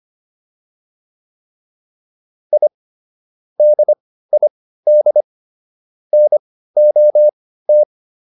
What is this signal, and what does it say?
Morse code, heard as a keyed tone, about 25 wpm, spaced as for 15 wpm, 600 Hz. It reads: I DID NOT
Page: intermediate-practice-files